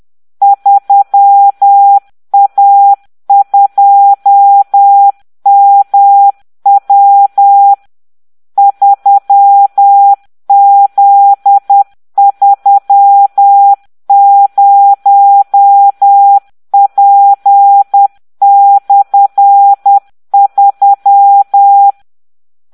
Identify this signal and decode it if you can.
Morse code, heard as a keyed tone, about 10 wpm, 800 Hz. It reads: 3A2MW 3Z30P/3